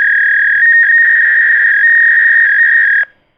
Fax machine handshake.
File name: Re Dragon / Release / Fax-machine-sound